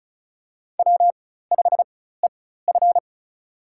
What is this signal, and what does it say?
Morse code, heard as a keyed tone, about 36 wpm, spaced as for 16 wpm, 700 Hz. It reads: W5EF
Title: USA Call Signs - 4x Character Spacing - 35wpm